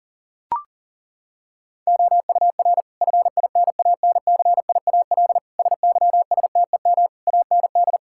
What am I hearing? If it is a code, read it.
Morse code, heard as a keyed tone, about 40 wpm, 690 Hz. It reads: OUR FINANCIAL SYSTEM AND
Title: Sentences from Top 1000 Words - Review Part 5 - 40wpm